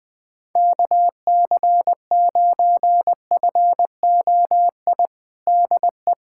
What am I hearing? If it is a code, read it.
Morse code, heard as a keyed tone, about 20 wpm, 700 Hz. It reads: KC9FOI DE
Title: USA Callsign DE Callsign - 6 Characters - 20wpm